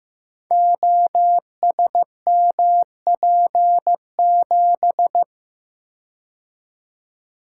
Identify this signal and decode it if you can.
Morse code, heard as a keyed tone, about 15 wpm, 700 Hz. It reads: OSMP7
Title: Code Groups - 5 Letter:Numbers - 15wpm